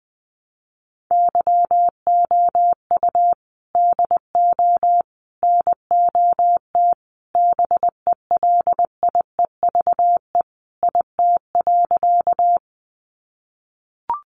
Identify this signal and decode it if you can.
Morse code, heard as a keyed tone, about 20 wpm, 700 Hz. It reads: YOU DO NOT BELIEVE IT.
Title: Sentences from Top 1000 Words 20wpm